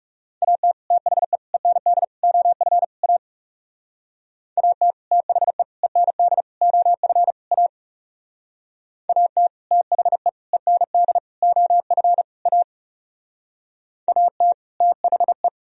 Morse code, heard as a keyed tone, about 35 wpm, 700 Hz. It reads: AT THE END OF A AT THE END OF A AT THE END OF A AT THE
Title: Speed-Racing - Sets of 5 Words 30wpm